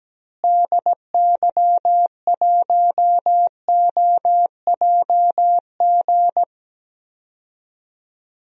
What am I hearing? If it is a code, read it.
Morse code, heard as a keyed tone, about 17 wpm, 700 Hz. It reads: DY1OJG
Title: DXCC-100 Call Signs - 6 Characters - Repeated 2x - 17wpm